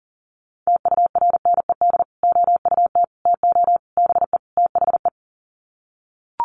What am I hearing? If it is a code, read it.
Morse code, heard as a keyed tone, about 40 wpm, 700 Hz. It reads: TURNED OUT TO BE THE
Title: Sets of 5 Words Encore 40wpm